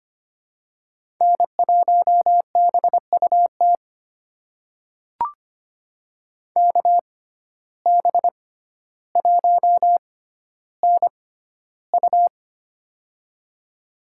Morse code, heard as a keyed tone, about 25 wpm, spaced as for 13 wpm, 700 Hz. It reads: N1BUT KB1NU
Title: USA Call Signs - 5 Characters - 6x Character Spacing - 25wpm